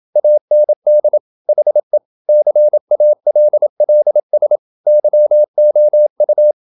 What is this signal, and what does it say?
Morse code, heard as a keyed tone, about 28 wpm, 600 Hz. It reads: AND HE CALLS YOU
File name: Never 0027wpm